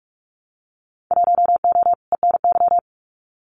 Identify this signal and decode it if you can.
Morse code, heard as a keyed tone, about 46 wpm, 700 Hz. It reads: 1O ENY
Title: Winter Field Day Exchanges 45wpm